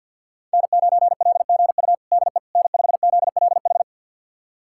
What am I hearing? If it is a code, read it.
Morse code, heard as a keyed tone, about 51 wpm, 700 Hz. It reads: N9LGU DE N5ZLH
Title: USA Callsign DE Callsign - 5 Characters - 50wpm